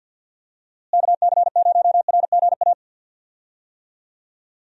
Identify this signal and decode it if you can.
Morse code, heard as a keyed tone, about 52 wpm, 700 Hz. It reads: KC0RGA
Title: USA Call Signs - 6 Characters - Repeated 2x - 50wpm